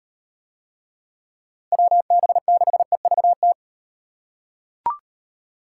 Morse code, heard as a keyed tone, about 38 wpm, 700 Hz. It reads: WB6EVT